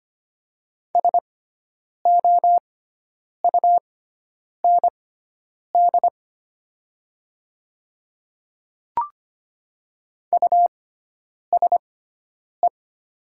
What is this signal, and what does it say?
Morse code, heard as a keyed tone, about 25 wpm, spaced as for 9 wpm, 700 Hz. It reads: SOUND USE
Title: 6x Character Spacing - Top 100 Words - 25wpm